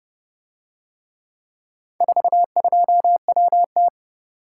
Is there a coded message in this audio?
Yes: Morse code from 2.0 to 3.9 s, heard as a keyed tone, about 28 wpm, 700 Hz. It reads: 42WT